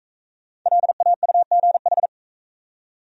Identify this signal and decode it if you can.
Morse code, heard as a keyed tone, about 41 wpm, 700 Hz. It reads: LAUGH